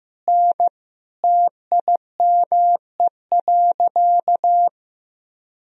Morse code, heard as a keyed tone, about 15 wpm, 700 Hz. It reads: N TIME.
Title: Sentences from Top 400 Words and Abbreviations 15wpm